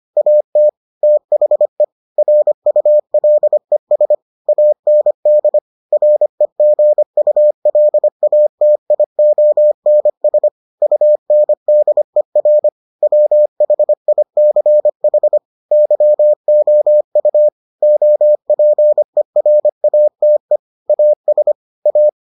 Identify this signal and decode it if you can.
Morse code, heard as a keyed tone, about 25 wpm, 600 Hz. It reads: AT THE RULES AND REGULATIONS UNDER WHICH YOU OPERATE AS A